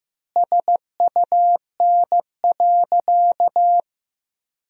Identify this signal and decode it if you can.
Morse code, heard as a keyed tone, about 15 wpm, 700 Hz. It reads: SUN.
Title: 3x Word-Spacing - Sentences from Top 300 Words - 15wpm